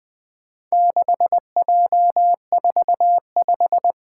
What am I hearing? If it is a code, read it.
Morse code, heard as a keyed tone, about 20 wpm, 700 Hz. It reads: 6J45
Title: Four Challenging Letter-Number Combinations 20wpm